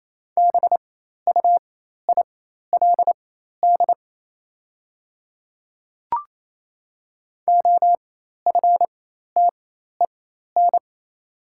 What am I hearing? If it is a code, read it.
Morse code, heard as a keyed tone, about 28 wpm, spaced as for 13 wpm, 700 Hz. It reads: BUILD OFTEN
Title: 4x Character Spacing - Top 300 Words - 28wpm